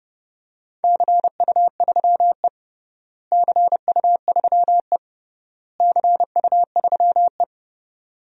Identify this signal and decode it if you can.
Morse code, heard as a keyed tone, about 30 wpm, 700 Hz. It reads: CU3E CU3E CU3E
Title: DXCC-100 Call Signs - 4 Characters - Repeated 3x - 30wpm